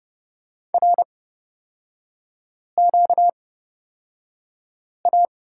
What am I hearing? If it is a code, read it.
Morse code, heard as a keyed tone, about 30 wpm, spaced as for 5 wpm, 700 Hz. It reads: RQA
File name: Lesson 38e - Letter q - Three Characters 30wpm